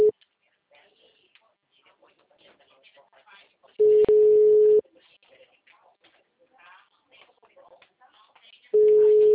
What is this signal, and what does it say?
Ringback tone (Europe, 425 Hz), with speech.
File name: out- -2009- - - .40176